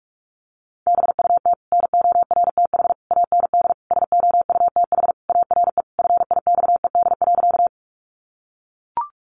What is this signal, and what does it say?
Morse code, heard as a keyed tone, about 44 wpm, 700 Hz. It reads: BUT NORTH AND SOUTH ARE FIXED.
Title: Sentences from Top 1100 Words 45wpm